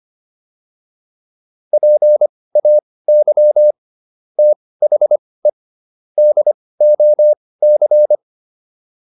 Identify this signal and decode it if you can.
Morse code, heard as a keyed tone, about 25 wpm, spaced as for 18 wpm, 600 Hz. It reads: PAY THE DOC
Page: intermediate-practice-files